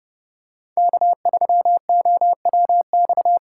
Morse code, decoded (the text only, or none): K3OWX